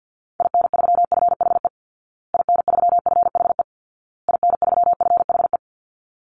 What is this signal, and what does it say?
Morse code, heard as a keyed tone, about 50 wpm, 700 Hz. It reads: IN3FHE IN3FHE IN3FHE
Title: Call Signs Repeated 2x - 50wpm